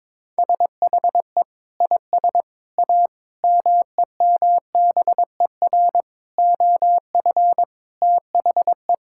Morse code, heard as a keyed tone, about 22 wpm, 700 Hz. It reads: SHE IS A MEMBER OF THE
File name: practice / 22wpm / Sentences with 3-Word Phrases - 22wpm